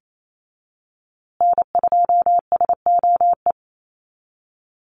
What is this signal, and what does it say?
Morse code, heard as a keyed tone, about 28 wpm, 700 Hz. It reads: N2SOE
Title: USA Call Signs Repeated 2x 28wpm